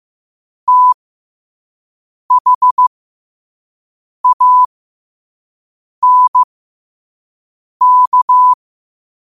Morse code, heard as a keyed tone, about 15 wpm, spaced as for 5 wpm, 1000 Hz. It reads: THANK